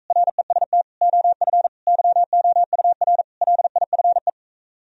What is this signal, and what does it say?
Morse code, heard as a keyed tone, about 44 wpm, 700 Hz. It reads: REST OF YOUR LIFE